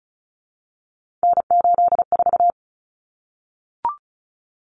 Morse code, heard as a keyed tone, about 34 wpm, 700 Hz. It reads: N84
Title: Common Cut Numbers - 3-Digit 35wpm